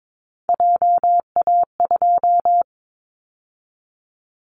Morse code, heard as a keyed tone, about 22 wpm, 700 Hz. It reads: JA2
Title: Callsign Prefixes for Top 50 DXCC - 22wpm